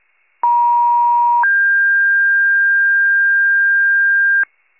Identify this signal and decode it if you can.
Two-tone page, heard as a one-second tone, then a three-second tone, about 950 then 1600 Hz, with noise. It is heard from 0.4 to 4.4 s.